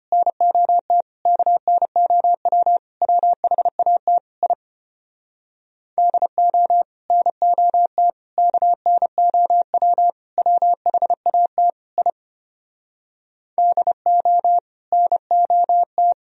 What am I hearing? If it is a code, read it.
Morse code, heard as a keyed tone, about 33 wpm, 700 Hz. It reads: NOT KNOW WHAT I DO NOT KNOW WHAT I DO NOT